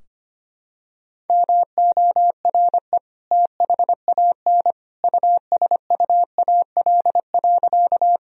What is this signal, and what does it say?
Morse code, heard as a keyed tone, about 25 wpm, 700 Hz. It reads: MORE THAN USUAL.
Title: Sentences from Top 1000 Words - Review Part 2 - 25wpm